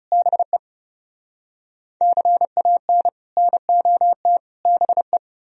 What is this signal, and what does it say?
Morse code, heard as a keyed tone, about 36 wpm, 700 Hz. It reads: BE CAN NOT BE
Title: Speed-Racing - Sets of 3 Words 30wpm